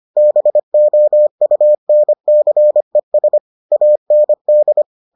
Morse code, heard as a keyed tone, about 25 wpm, 600 Hz. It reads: BOUNCES AND